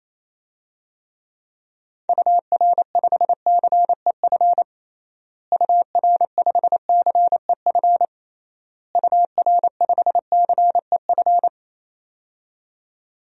Morse code, heard as a keyed tone, about 28 wpm, 700 Hz. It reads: UR5CEF UR5CEF UR5CEF
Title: DXCC-100 Call Signs - 6 Characters - Repeated 3x - 28wpm